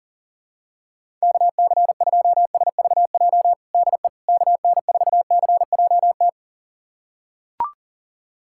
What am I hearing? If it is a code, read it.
Morse code, heard as a keyed tone, about 37 wpm, 700 Hz. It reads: KC2SVJ DE KN4CJT
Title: USA Callsign DE Callsign - 6 Characters - 40wpm